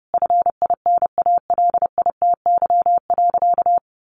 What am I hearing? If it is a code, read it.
Morse code, heard as a keyed tone, about 30 wpm, 700 Hz. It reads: FINALITY.
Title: A Sound Of Thunder 30wpm